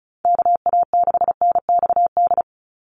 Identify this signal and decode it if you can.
Morse code, heard as a keyed tone, about 34 wpm, 700 Hz. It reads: KA6NXD
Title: USA Call Signs 35wpm